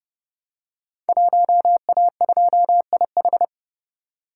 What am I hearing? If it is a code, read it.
Morse code, heard as a keyed tone, about 30 wpm, 700 Hz. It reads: 1A2IH
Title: Code Groups - 5 Letter:Numbers - 30wpm